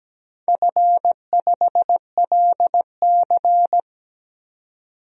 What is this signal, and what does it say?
Morse code, heard as a keyed tone, about 17 wpm, 700 Hz. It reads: F5LC
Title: DXCC-100 Call Signs - 4 Characters - 17wpm